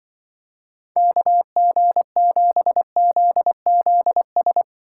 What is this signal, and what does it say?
Morse code, heard as a keyed tone, about 24 wpm, 700 Hz. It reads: KG7ZZS
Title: Speed Racing - USA 2x3 Call Signs 20wpm